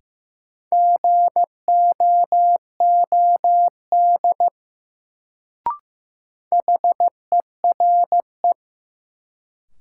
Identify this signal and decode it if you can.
Morse code, heard as a keyed tone, about 15 wpm, 700 Hz. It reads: GOOD HERE
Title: Sentences from Top 200 Words - New Word Drill - 15wpm